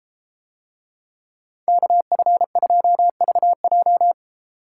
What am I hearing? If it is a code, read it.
Morse code, heard as a keyed tone, about 31 wpm, 700 Hz. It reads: KF2VJ